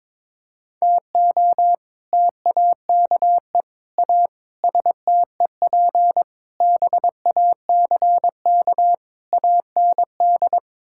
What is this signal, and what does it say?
Morse code, heard as a keyed tone, about 22 wpm, 700 Hz. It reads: TO TAKE A STEP BACK AND
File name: practice / 22wpm / Sets of 6 Words - 22wpm